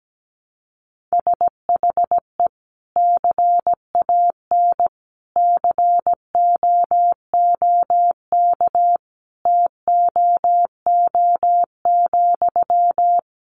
Morse code, heard as a keyed tone, about 17 wpm, 700 Hz. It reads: SHE CAN COOK TOO,
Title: Sentences from Top 700 Words 17wpm